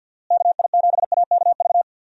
Morse code, heard as a keyed tone, about 51 wpm, 700 Hz. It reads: KI7AKV
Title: USA Call Signs - 6 Characters - Repeated 2x - 50wpm